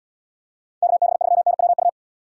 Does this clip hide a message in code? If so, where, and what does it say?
Yes: Morse code from 0.8 to 1.9 s, heard as a keyed tone, about 72 wpm, 700 Hz. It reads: KB3NWU